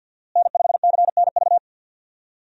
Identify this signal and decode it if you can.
Morse code, heard as a keyed tone, about 47 wpm, 700 Hz. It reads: N5CNV
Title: USA Call Signs - 5 Characters - Repeated 3x - 50wpm